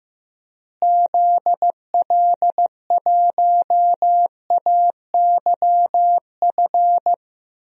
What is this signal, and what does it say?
Morse code, heard as a keyed tone, about 16 wpm, 700 Hz. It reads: ZL1AYF
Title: DXCC-100 Call Signs - 6 Characters - Repeated 3x - 15wpm